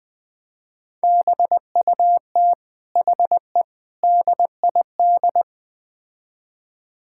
Morse code, heard as a keyed tone, about 20 wpm, 700 Hz. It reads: BUT HE DID
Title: Build A Phrase - 4-Word Phrases - 20wpm